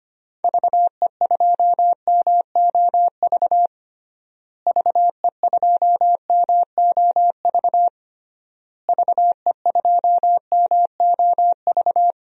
Morse code, heard as a keyed tone, about 25 wpm, 700 Hz. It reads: VE2MOV VE2MOV VE2MOV
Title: DXCC-100 Call Signs - 6 Characters - Repeated 3x - 25wpm